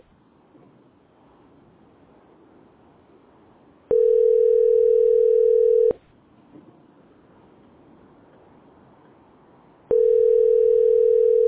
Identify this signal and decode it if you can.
Ringback tone (North America), 440 + 480 Hz.